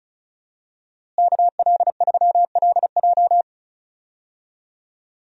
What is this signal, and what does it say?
Morse code, heard as a keyed tone, about 36 wpm, 700 Hz. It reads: KL3LJ
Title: USA Call Signs - 5 Characters - Repeated 2x - 35wpm